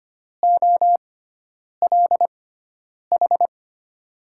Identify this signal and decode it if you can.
Morse code, heard as a keyed tone, about 25 wpm, spaced as for 9 wpm, 700 Hz. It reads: OLH
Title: USA Call Signs - 6x Character Spacing - 25wpm